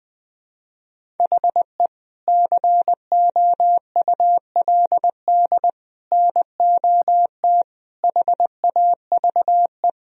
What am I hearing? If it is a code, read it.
Morse code, heard as a keyed tone, about 20 wpm, 700 Hz. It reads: HE COULD NOT HAVE
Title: Sentences from Top 1000 Words - Review Part 3 - 20wpm